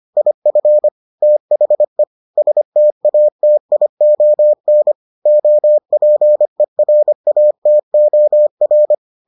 Morse code, heard as a keyed tone, about 25 wpm, 600 Hz. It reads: IF THE STATION OPERATOR